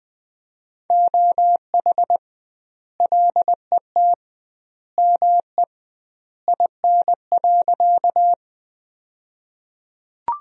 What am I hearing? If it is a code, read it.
Morse code, heard as a keyed tone, about 20 wpm, 700 Hz. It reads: OH LET ME IN.
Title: 2x Word-Spacing - Sentences from Top 500 Words - 20wpm